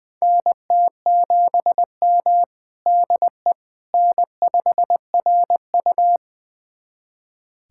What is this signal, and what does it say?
Morse code, heard as a keyed tone, about 20 wpm, 700 Hz. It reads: NT7M DE N5RU
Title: USA Callsign DE Callsign - 4 Characters - 20wpm